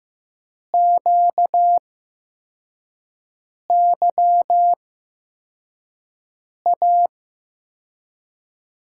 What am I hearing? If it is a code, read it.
Morse code, heard as a keyed tone, about 15 wpm, spaced as for 4 wpm, 700 Hz. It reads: QYA
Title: USA Call Signs - 6 Characters - 8x Character Spacing - 15wpm